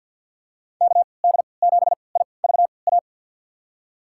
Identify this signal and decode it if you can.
Morse code, heard as a keyed tone, about 45 wpm, spaced as for 27 wpm, 700 Hz. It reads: KD7IVA
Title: USA Call Signs - 6 Characters - 3x Character Spacing - 50wpm